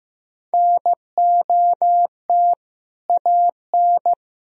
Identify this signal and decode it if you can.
Morse code, heard as a keyed tone, about 15 wpm, 700 Hz. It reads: NOT AN
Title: Sentences from Top 1000 Words - Review Part 3 - 15wpm